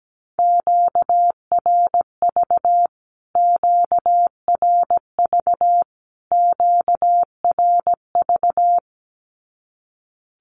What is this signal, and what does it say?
Morse code, heard as a keyed tone, about 17 wpm, 700 Hz. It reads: QRV QRV QRV
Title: Q-Codes Repeated 3x 17wpm